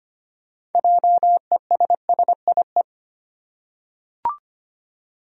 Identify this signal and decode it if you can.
Morse code, heard as a keyed tone, about 25 wpm, 700 Hz. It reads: JESSIE
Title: USA Common Names - 25wpm